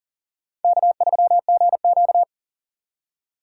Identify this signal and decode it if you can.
Morse code, heard as a keyed tone, about 40 wpm, 700 Hz. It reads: K3GQ
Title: USA Call Signs - 3x Character Spacing - 40wpm